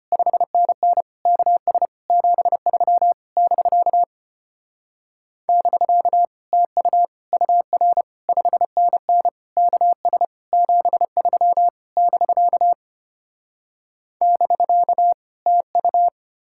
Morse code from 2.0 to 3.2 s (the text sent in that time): 73